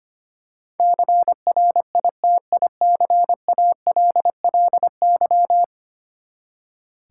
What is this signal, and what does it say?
Morse code, heard as a keyed tone, about 25 wpm, 700 Hz. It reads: CRITICALLY
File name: Ten Letter Words - 25wpm